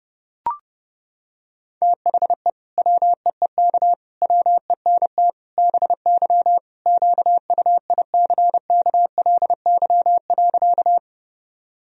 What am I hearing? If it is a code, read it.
Morse code, heard as a keyed tone, about 30 wpm, 700 Hz. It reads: THE WEEK WENT BY QUICKLY.
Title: Sentences from Top 1000 Words - Review Part 2 - 30wpm